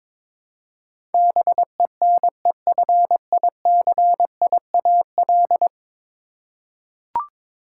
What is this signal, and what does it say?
Morse code, heard as a keyed tone, about 22 wpm, 700 Hz. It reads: BENEFICIAL
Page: morse-code-ninja-practice-22wpm